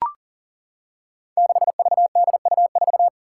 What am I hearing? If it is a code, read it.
Morse code, heard as a keyed tone, about 40 wpm, 700 Hz. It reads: 6VDU4
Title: Five Challenging Letter-Number Combinations 40wpm